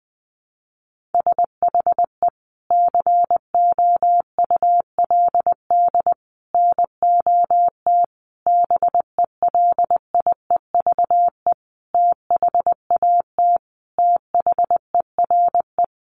Morse code, heard as a keyed tone, about 20 wpm, 700 Hz. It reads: SHE COULD NOT BELIEVE THAT THERE
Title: Sentences from Top 1000 Words 20wpm